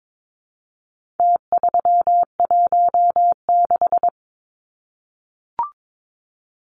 Morse code, heard as a keyed tone, about 22 wpm, 700 Hz. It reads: T316
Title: Numbers - Contest Style - 4-Digit 22wpm